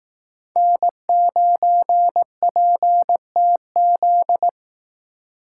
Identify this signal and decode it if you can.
Morse code, heard as a keyed tone, about 18 wpm, 700 Hz. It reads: N9PTZ
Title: Speed Racing - USA 1x3 Call Signs 15wpm